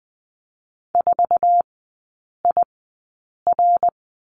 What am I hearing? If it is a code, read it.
Morse code, heard as a keyed tone, about 20 wpm, spaced as for 8 wpm, 700 Hz. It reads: 4IR
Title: 114 Lesson 19e - Letter w - Three Characters 20wpm